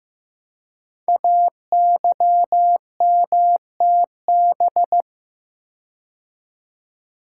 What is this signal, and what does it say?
Morse code, heard as a keyed tone, about 15 wpm, 700 Hz. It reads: AYMTB
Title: Code Groups - 5 Letters - 15wpm